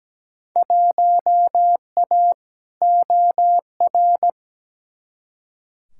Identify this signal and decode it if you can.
Morse code, heard as a keyed tone, about 17 wpm, 700 Hz. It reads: 1A OR
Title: Field Day Exchanges - 17wpm